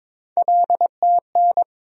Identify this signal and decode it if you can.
Morse code, heard as a keyed tone, about 23 wpm, 700 Hz. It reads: LTN
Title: USA Call Sign Postfixes - 3 Letters 22wpm